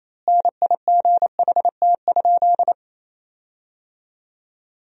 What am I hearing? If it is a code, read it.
Morse code, heard as a keyed tone, about 28 wpm, 700 Hz. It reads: NIGHT?
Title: 3x Word-Spacing - Sentences from Top 300 Words - 28wpm